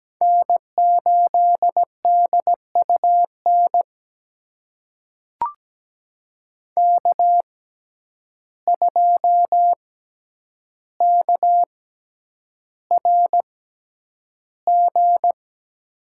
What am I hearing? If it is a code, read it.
Morse code, heard as a keyed tone, about 17 wpm, spaced as for 9 wpm, 700 Hz. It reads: N8DUN K2KRG
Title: USA Call Signs - 5 Characters - 6x Character Spacing - 17wpm